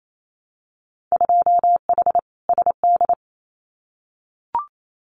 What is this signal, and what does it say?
Morse code, heard as a keyed tone, about 28 wpm, 700 Hz. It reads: 2H SD
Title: Winter Field Day Exchanges 28wpm